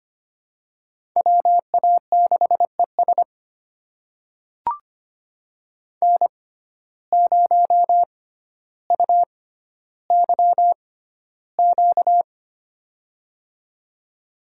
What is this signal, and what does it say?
Morse code, heard as a keyed tone, about 25 wpm, spaced as for 13 wpm, 700 Hz. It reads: WA6ES N0UYQ
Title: USA Call Signs - 5 Characters - 6x Character Spacing - 25wpm